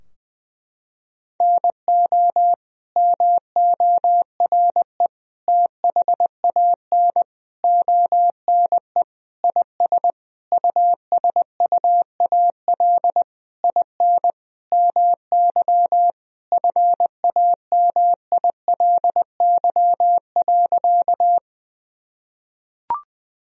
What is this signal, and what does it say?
Morse code, heard as a keyed tone, about 20 wpm, 700 Hz. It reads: NO MORE THAN ONE IS USUAL IN MY FAMILY.
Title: Sentences from Top 1000 Words - Review Part 2 - 20wpm